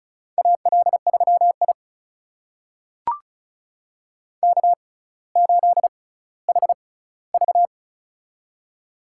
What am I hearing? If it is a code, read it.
Morse code, heard as a keyed tone, about 36 wpm, spaced as for 18 wpm, 700 Hz. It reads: AL3I K8HV
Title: USA Call Signs - 4 Characters - 6x Character Spacing - 35wpm